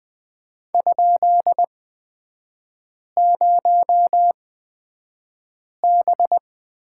Morse code, heard as a keyed tone, about 20 wpm, spaced as for 5 wpm, 690 Hz. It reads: ?0B
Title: 229 Lesson 37e - Letter x - Three Characters 20wpm